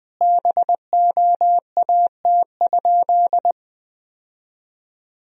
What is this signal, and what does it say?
Morse code, heard as a keyed tone, about 20 wpm, 700 Hz. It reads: BOAT?
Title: Sentences from Top 1000 Words - Review Part 3 - 20wpm